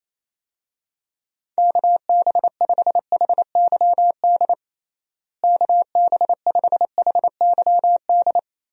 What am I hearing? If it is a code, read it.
Morse code, heard as a keyed tone, about 28 wpm, 700 Hz. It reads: KB5HYD KB5HYD
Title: USA Call Signs - 6 Characters - Repeated 2x - 28wpm